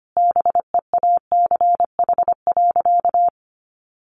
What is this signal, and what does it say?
Morse code, heard as a keyed tone, about 25 wpm, 700 Hz. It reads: BEACH.
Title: Sentences from Top 1500 Words 25wpm